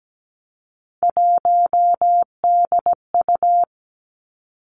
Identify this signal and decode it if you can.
Morse code, heard as a keyed tone, about 17 wpm, 700 Hz. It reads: 1DU
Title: Three Challenging Letter-Number Combinations 17wpm